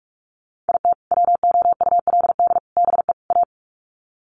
Morse code, heard as a keyed tone, about 44 wpm, 700 Hz. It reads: IT WOULD BE A
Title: Sets of 4 Words 45wpm